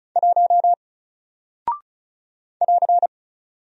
Morse code, heard as a keyed tone, about 36 wpm, 700 Hz. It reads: 1 <AR>
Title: Single Character (Letter:Number:Punctuation:Prosign) Alt 35wpm